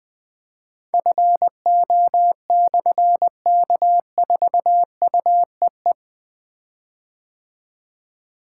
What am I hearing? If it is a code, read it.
Morse code, heard as a keyed tone, about 21 wpm, 700 Hz. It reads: FO/K4UEE
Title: Difficult Call Signs 20wpm